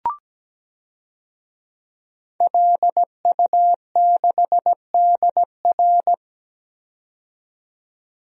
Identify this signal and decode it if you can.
Morse code, heard as a keyed tone, about 17 wpm, 700 Hz. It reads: LU6DR
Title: DXCC-100 Call Signs - 5 Characters - 17wpm